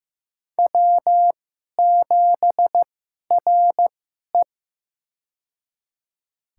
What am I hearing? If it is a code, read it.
Morse code, heard as a keyed tone, about 15 wpm, spaced as for 11 wpm, 700 Hz. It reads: W7RE
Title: USA Call Signs - 2x Character Spacing - 15wpm